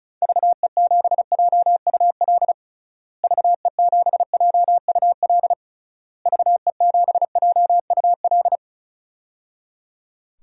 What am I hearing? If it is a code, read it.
Morse code, heard as a keyed tone, about 34 wpm, 700 Hz. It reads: VE7JUL VE7JUL VE7JUL
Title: DXCC-100 Call Signs - 6 Characters - Repeated 3x - 35wpm